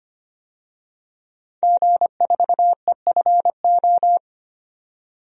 Morse code, heard as a keyed tone, about 25 wpm, 700 Hz. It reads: G4EFO
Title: DXCC-100 Call Signs - 5 Characters - 25wpm